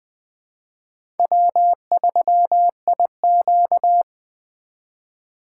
Morse code, heard as a keyed tone, about 20 wpm, 700 Hz. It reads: W3IQ
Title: USA Call Signs - 4 Characters - 4x Character Spacing - 20wpm